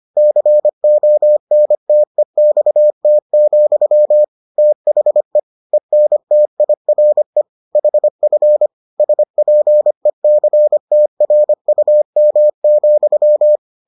Morse code, heard as a keyed tone, about 25 wpm, 600 Hz. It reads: CONTEXT, THE ENTIRE HF SPECTRUM,